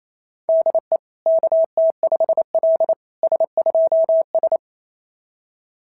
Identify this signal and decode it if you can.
Morse code, heard as a keyed tone, about 25 wpm, 650 Hz. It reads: DE KT5L S2S
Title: SOTA QSO Practice 28wpm